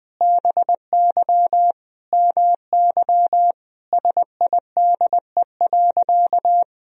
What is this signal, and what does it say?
Morse code, heard as a keyed tone, about 20 wpm, 700 Hz. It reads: BY MY SIDE.
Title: Sentences from Top 1000 Words - Review Part 1 - 20wpm